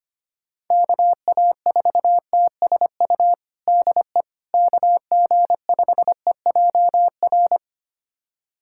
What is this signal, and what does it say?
Morse code, heard as a keyed tone, about 25 wpm, 700 Hz. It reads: KA4TSU DE KG5EJR
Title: USA Callsign DE Callsign - 6 Characters - 25wpm